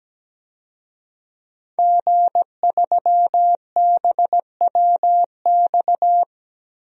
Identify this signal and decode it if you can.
Morse code, heard as a keyed tone, about 17 wpm, 700 Hz. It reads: G3BWX
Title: DXCC-100 Call Signs - 5 Characters - Repeated 3x - 17wpm